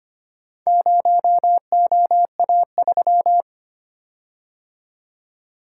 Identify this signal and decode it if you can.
Morse code, heard as a keyed tone, about 25 wpm, 690 Hz. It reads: 0OA3
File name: practice / 25wpm / Four Letter-Number Combinations - 25wpm